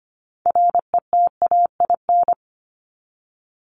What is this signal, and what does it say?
Morse code, heard as a keyed tone, about 25 wpm, 700 Hz. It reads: RETAIN
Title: Top 1751-2000 Words 25wpm